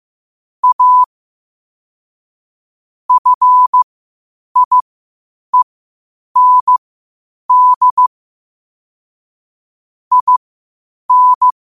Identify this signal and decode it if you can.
Morse code, heard as a keyed tone, about 15 wpm, spaced as for 8 wpm, 1000 Hz. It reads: A FIEND IN
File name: qotdfiles / 8wpm